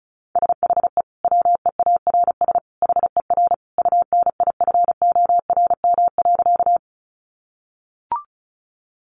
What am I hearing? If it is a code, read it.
Morse code, heard as a keyed tone, about 34 wpm, 700 Hz. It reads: SHE WEARS HER UNIFORM.
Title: Easy Phrases 35wpm